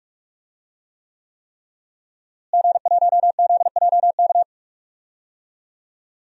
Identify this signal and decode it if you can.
Morse code, heard as a keyed tone, about 46 wpm, 700 Hz. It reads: G1ZJK
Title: DXCC-100 Call Signs - 5 Characters - Repeated 3x - 45wpm